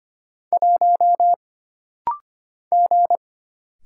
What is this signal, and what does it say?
Morse code, heard as a keyed tone, about 25 wpm, 700 Hz. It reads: 1 G
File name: Single Character (Letter:Number:Punctuation:Prosign) - Rapid-Fire - 25wpm